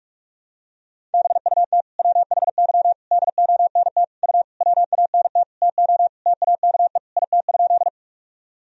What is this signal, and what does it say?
Morse code, heard as a keyed tone, about 46 wpm, 700 Hz. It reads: BUT WHY DONT U WANT TO TAKE IT?
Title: Sentences from Top 300 Words and Abbreviations 45wpm